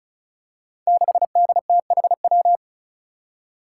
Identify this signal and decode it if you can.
Morse code, heard as a keyed tone, about 36 wpm, 700 Hz. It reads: 6DTHW
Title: Five Challenging Letter-Number Combinations 35wpm